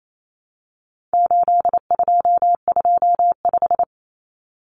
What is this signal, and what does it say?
Morse code, heard as a keyed tone, about 28 wpm, 700 Hz. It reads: 8225
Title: Common Cut Numbers - 4-Digit 28wpm